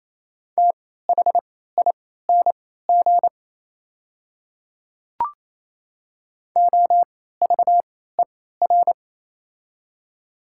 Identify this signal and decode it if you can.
Morse code, heard as a keyed tone, about 28 wpm, spaced as for 16 wpm, 700 Hz. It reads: THING OVER
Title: 3x Character Spacing - Top 100 Words - 28wpm